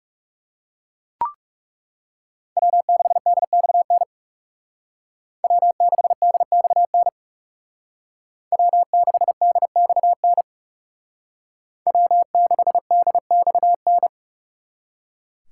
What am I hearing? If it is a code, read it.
Morse code, heard as a keyed tone, about 37 wpm, 700 Hz. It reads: W6DXN W6DXN W6DXN W6DXN
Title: Speed Racing - USA 1x3 Call Signs 30wpm